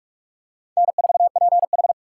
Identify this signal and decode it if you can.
Morse code, heard as a keyed tone, about 44 wpm, 690 Hz. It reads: N4PH
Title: USA Call Signs - 4 Characters - 4x Character Spacing - 45wpm